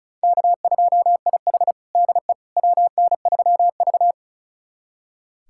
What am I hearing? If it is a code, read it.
Morse code, heard as a keyed tone, about 36 wpm, 700 Hz. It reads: K2IH DE WN3V
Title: USA Callsign DE Callsign - 4 Characters - 35wpm